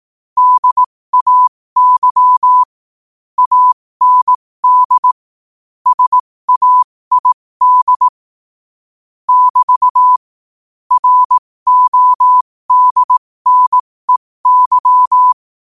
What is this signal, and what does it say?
Morse code, heard as a keyed tone, about 18 wpm, spaced as for 15 wpm, 1000 Hz. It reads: DAY AND SAID <BT> RODNEY
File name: 15wpm